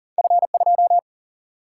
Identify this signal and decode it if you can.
Morse code, heard as a keyed tone, about 40 wpm, 700 Hz. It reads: F2
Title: Single-Letter Single-Number 40wpm